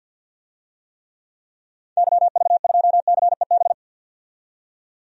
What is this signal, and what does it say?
Morse code, heard as a keyed tone, about 46 wpm, 700 Hz. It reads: YV2CEB
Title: DXCC-100 Call Signs - 6 Characters - 50wpm